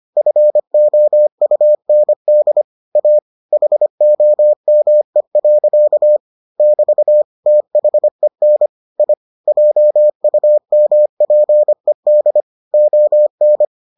Morse code, heard as a keyed tone, about 25 wpm, 600 Hz. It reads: FOUND A HOME. <BT> THEN I JUMPED ON